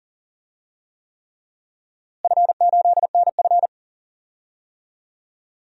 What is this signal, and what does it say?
Morse code, heard as a keyed tone, about 40 wpm, 700 Hz. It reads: F8NF
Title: DXCC-100 Call Signs - 4 Characters - Repeated 3x - 40wpm